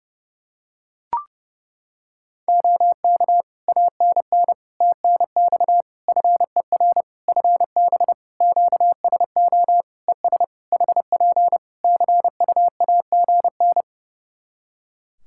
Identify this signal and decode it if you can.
Morse code, heard as a keyed tone, about 30 wpm, 700 Hz. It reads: OK ANN TNX FER FB QSO ES HP CUAGN
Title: Practice QSO - Exchange 4 - Ending - 30wpm